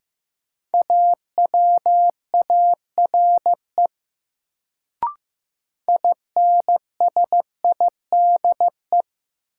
Morse code, heard as a keyed tone, about 15 wpm, 700 Hz. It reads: AWARE INSIDE
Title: Sentences from Top 1200 Words - New Word Drill Alt - 15wpm